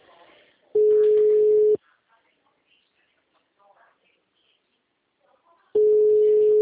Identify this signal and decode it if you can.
Ringback tone (Europe, 425 Hz), with speech.
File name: out- -2008- - - .45